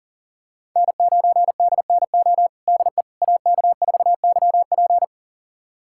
Morse code, heard as a keyed tone, about 40 wpm, 700 Hz. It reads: N9DNO DE AK4YP
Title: USA Callsign DE Callsign - 5 Characters - 40wpm